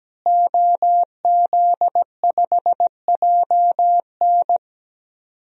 Morse code, heard as a keyed tone, about 17 wpm, 700 Hz. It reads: OZ5JN
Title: DXCC-100 Call Signs - 5 Characters - Repeated 3x - 17wpm